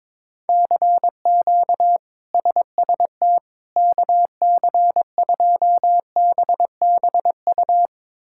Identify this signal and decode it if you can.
Morse code, heard as a keyed tone, about 22 wpm, 700 Hz. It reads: CQ SST KC2BBU
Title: K1USN SST QSO Practice - 22wpm